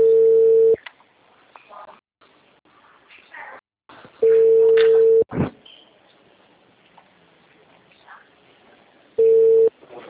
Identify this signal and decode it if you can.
Ringback tone (China, 450 Hz).
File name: out- -2015- - - .241